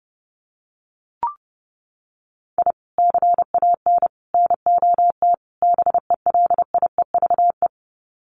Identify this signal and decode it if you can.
Morse code, heard as a keyed tone, about 30 wpm, 700 Hz. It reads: I CAN NOT BELIEVE
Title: Sets of 4 Words 30wpm